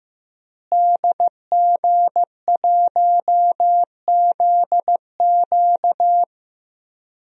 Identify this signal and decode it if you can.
Morse code, heard as a keyed tone, about 15 wpm, 700 Hz. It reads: DG1ZQ
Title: DXCC-100 Call Signs - 5 Characters - Repeated 2x - 15wpm